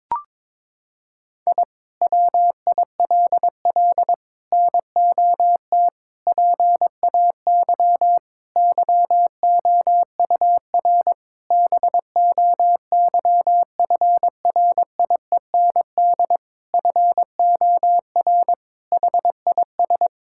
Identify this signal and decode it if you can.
Morse code, heard as a keyed tone, about 22 wpm, 700 Hz. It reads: I WILL NOT PAY YOUR BOYFRIEND FOR HIS
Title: Sentences from Top 1000 Words - Review Part 5 - 22wpm